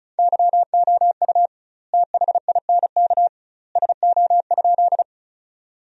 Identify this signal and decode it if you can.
Morse code, heard as a keyed tone, about 34 wpm, 700 Hz. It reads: YOU THINK SO?
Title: 2x Word-Spacing - Sentences from Top 200 Words - 35wpm